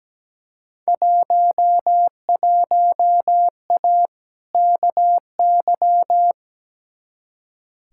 Morse code, heard as a keyed tone, about 17 wpm, 700 Hz. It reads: 11A KY
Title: Field Day Exchanges Encore - 17wpm